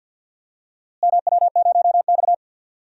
Morse code, heard as a keyed tone, about 49 wpm, 700 Hz. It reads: MW0X
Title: DXCC-100 Call Signs - 4 Characters - Repeated 2x - 50wpm